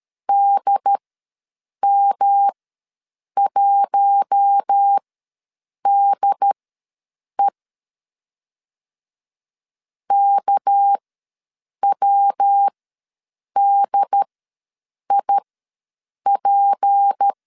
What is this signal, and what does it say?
Morse code, heard as a keyed tone, about 13 wpm, spaced as for 7 wpm, 800 Hz. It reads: DM1DE KWDIP
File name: LEZIONI / 6 PKX0 / 6 PK ws 12wpm 3116